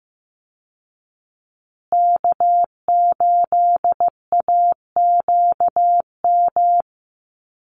Morse code, heard as a keyed tone, about 15 wpm, 700 Hz. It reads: K8AQM
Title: Call Signs Repeated 3x - 15wpm